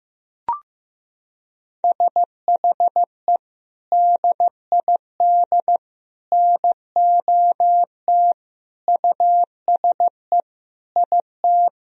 Morse code, heard as a keyed tone, about 16 wpm, 700 Hz. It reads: SHE DID NOT USE IT
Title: Sentences from Top 1000 Words - Review Part 1 - 15wpm